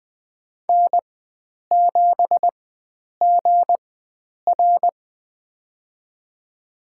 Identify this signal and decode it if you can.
Morse code, heard as a keyed tone, about 20 wpm, spaced as for 9 wpm, 700 Hz. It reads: N7GR
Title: USA Call Signs - 4 Characters - 4x Character Spacing - 20wpm